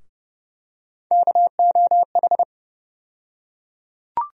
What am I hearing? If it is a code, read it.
Morse code, heard as a keyed tone, about 30 wpm, 700 Hz. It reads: KOH
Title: USA Call Sign Postfixes - 3 Letters 30wpm